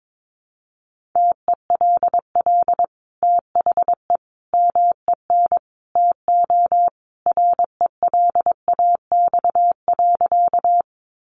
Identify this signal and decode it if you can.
Morse code, heard as a keyed tone, about 22 wpm, 700 Hz. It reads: TELL THE MEN TO RELAX.
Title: Sentences from Top 1400 Words 22wpm